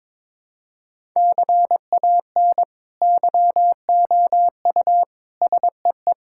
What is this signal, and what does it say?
Morse code, heard as a keyed tone, about 22 wpm, 700 Hz. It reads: CAN YOU SEE
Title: Sentences from Top 1000 Words - Review Part 6 - 22wpm